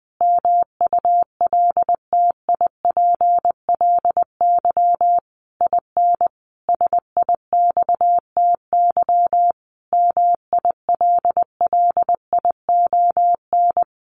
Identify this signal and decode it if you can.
Morse code, heard as a keyed tone, about 20 wpm, 700 Hz. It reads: MULTIPLY IN SIXTY MILLION
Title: A Sound Of Thunder 20wpm